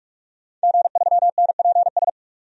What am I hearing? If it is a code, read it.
Morse code, heard as a keyed tone, about 42 wpm, 700 Hz. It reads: G3NPS
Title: DXCC-100 Call Signs - 5 Characters - 45wpm